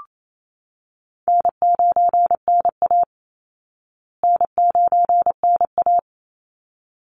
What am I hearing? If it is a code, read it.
Morse code, heard as a keyed tone, about 28 wpm, 700 Hz. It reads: N9NA N9NA
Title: Easy Call Signs Repeated 2x - 28wpm